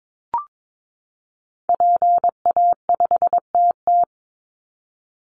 Morse code, heard as a keyed tone, about 22 wpm, 700 Hz. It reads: PA5TT
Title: Call Signs Repeated 2x - 22wpm